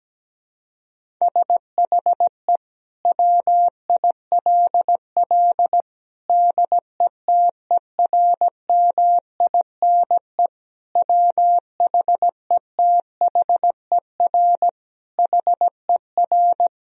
Morse code, heard as a keyed tone, about 17 wpm, 700 Hz. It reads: SHE WILL DETERMINE WHETHER HER
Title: Sentences from Top 1000 Words - Review Part 4 - 17wpm